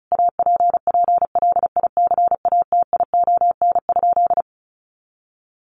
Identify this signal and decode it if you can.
Morse code, heard as a keyed tone, about 34 wpm, 700 Hz. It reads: APPLICATION?
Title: Sentences from Top 1200 Words 35wpm